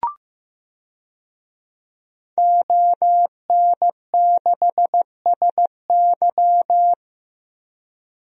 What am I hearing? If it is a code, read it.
Morse code, heard as a keyed tone, about 15 wpm, 700 Hz. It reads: ON6SY
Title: DXCC-100 Call Signs - 5 Characters - 15wpm